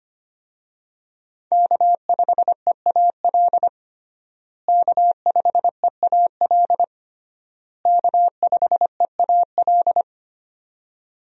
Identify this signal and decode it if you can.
Morse code, heard as a keyed tone, about 25 wpm, 700 Hz. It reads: K5EAL K5EAL K5EAL
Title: USA Call Signs - 5 Characters - Repeated 3x - 25wpm